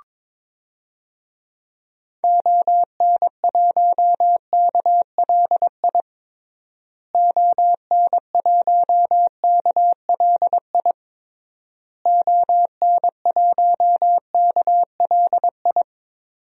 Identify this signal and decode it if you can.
Morse code, heard as a keyed tone, about 22 wpm, 700 Hz. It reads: ON1KLI ON1KLI ON1KLI
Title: DXCC-100 Call Signs - 6 Characters - Repeated 3x - 22wpm